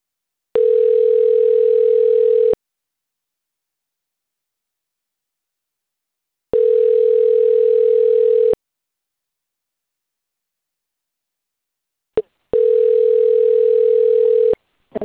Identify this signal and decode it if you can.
Ringback tone (North America), 440 + 480 Hz.